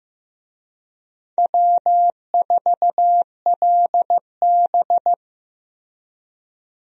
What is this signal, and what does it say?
Morse code, heard as a keyed tone, about 15 wpm, 700 Hz. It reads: W4LB
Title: USA Call Signs - 4 Characters - Repeated 2x - 15wpm